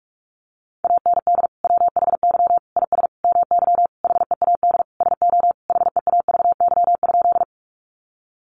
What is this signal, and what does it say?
Morse code, heard as a keyed tone, about 44 wpm, 700 Hz. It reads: AND WHY IS MY HEAD SO HEAVY?
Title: Sentences from Top 1100 Words 45wpm